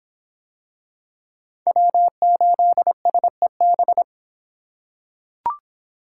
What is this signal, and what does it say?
Morse code, heard as a keyed tone, about 24 wpm, 700 Hz. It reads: W8SEB